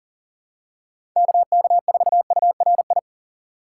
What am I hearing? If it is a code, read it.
Morse code, heard as a keyed tone, about 40 wpm, 700 Hz. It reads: KK4URI
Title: USA Call Signs - 6x Character Spacing - 40wpm